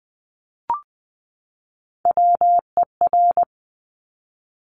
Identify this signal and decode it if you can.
Morse code, heard as a keyed tone, about 20 wpm, 700 Hz. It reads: WER
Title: Top Three Letter Combinations in Words 20wpm